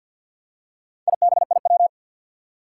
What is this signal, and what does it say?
Morse code, heard as a keyed tone, about 47 wpm, 700 Hz. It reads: IBIW
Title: Four Challenging Letter-Number Combinations 50wpm